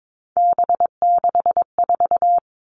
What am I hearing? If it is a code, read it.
Morse code, heard as a keyed tone, about 22 wpm, 700 Hz. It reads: B64
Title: All Cut Numbers - 3-Digit 22wpm